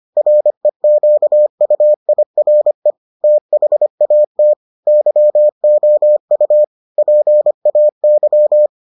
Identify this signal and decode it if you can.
Morse code, heard as a keyed tone, about 25 wpm, 600 Hz. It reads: REQUIRE THAT YOU PAY